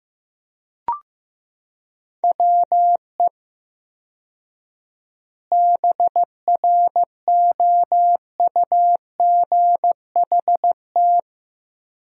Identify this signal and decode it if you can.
Morse code, heard as a keyed tone, about 15 wpm, 700 Hz. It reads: WE BROUGHT
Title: 4x Word-Spacing - Sentences from Top 500 Words - 15wpm